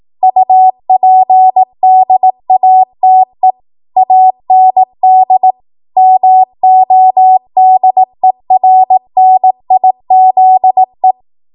Morse code, heard as a keyed tone, about 18 wpm, 750 Hz. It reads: UPDATE AND MODERNIZE
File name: Morse / 18WPM